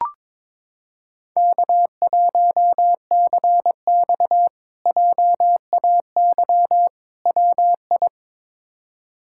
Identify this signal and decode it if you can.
Morse code, heard as a keyed tone, about 21 wpm, 700 Hz. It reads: K1CX JAY WI
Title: K1USN SST QSO Practice - 22wpm